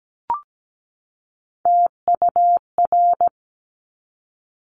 Morse code, heard as a keyed tone, about 17 wpm, 700 Hz. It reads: TUR
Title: Top Three Letter Combinations in Words 17wpm